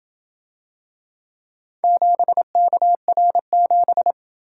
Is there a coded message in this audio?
Yes: Morse code from 1.8 to 4.1 s, heard as a keyed tone, about 27 wpm, 700 Hz. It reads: 7KR7